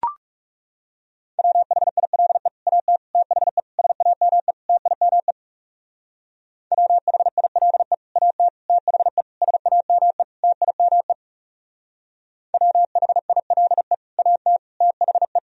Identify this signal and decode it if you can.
Morse code, heard as a keyed tone, about 35 wpm, 700 Hz. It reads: WHILE AT THE SAME TIME WHILE AT THE SAME TIME WHILE AT THE
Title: Speed-Racing - Sets of 5 Words 30wpm